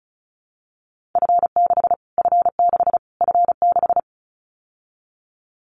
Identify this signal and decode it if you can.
Morse code, heard as a keyed tone, about 34 wpm, 700 Hz. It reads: F6 F6 F6
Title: Callsign Prefixes for Top 25 DXCC - Repeated 3x - 35wpm